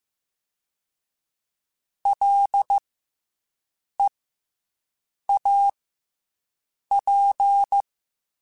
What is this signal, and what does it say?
Morse code, heard as a keyed tone, about 15 wpm, 800 Hz. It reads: L E A P
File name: Special INGs-mp3s / Special INGs-mp3s / Fast ing @13